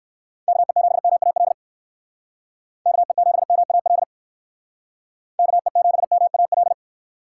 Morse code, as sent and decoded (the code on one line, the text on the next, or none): -.- . --... -- .- .-.. / -.- . --... -- .- .-.. / -.- . --... -- .- .-..
KE7MAL KE7MAL KE7MAL